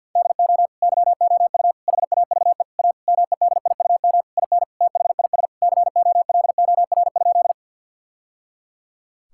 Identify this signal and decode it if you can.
Morse code, heard as a keyed tone, about 50 wpm, 700 Hz. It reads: DO YOU HAVE A MEDIUM IN THIS COLOR?
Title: Sentences from Top 1000 Words - Review Part 6 - 50wpm